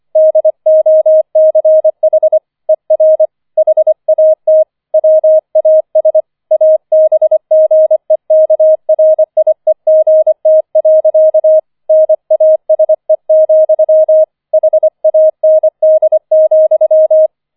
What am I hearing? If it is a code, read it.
Morse code, heard as a keyed tone, about 25 wpm, 630 Hz. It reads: DOCH ER HAT WAS ABGEKRIEGT. NASE, HAND,
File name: 004 Vierter Streich 25wpm